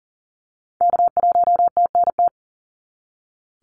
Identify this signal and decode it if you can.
Morse code, heard as a keyed tone, about 42 wpm, 700 Hz. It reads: K1TNT
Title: NA SOTA Activator Callsigns 40wpm